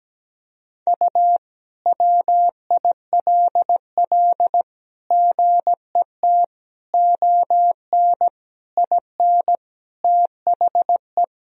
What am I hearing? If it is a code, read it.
Morse code, heard as a keyed tone, about 17 wpm, 700 Hz. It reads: U WILL GET ON IN THE
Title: Sentences from Top 200 Words and Abbreviations 17wpm